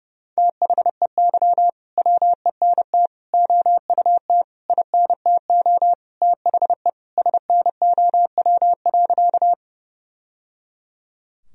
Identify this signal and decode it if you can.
Morse code, heard as a keyed tone, about 32 wpm, 700 Hz. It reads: THEY WENT OUT INTO THE SNOW.
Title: Sentences from Top 1000 Words - Review Part 3 - 30wpm